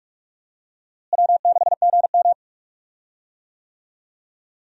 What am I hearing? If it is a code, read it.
Morse code, heard as a keyed tone, about 42 wpm, 700 Hz. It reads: W6GM
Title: USA Call Signs - 4 Characters - Repeated 2x - 45wpm